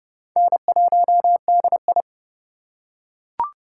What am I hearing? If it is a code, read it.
Morse code, heard as a keyed tone, about 31 wpm, 700 Hz. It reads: N1DI
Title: USA Call Signs - 4 Characters - 2x Character Spacing - 30wpm